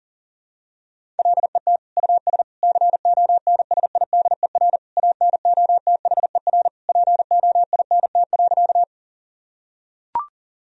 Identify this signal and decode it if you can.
Morse code, heard as a keyed tone, about 40 wpm, 700 Hz. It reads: LET US CONSIDER ANOTHER POINT.
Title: Sentences from Top 1000 Words - Review Part 4 - 40wpm